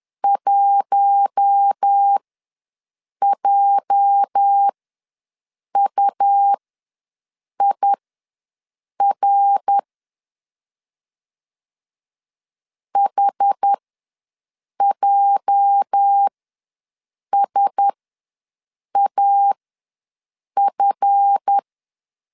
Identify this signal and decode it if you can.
Morse code, heard as a keyed tone, about 10 wpm, spaced as for 6 wpm, 800 Hz. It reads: 1JUIR HJSAF
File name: LEZIONI / 8 YJ45 / 8 YJ ws 10wpm 4043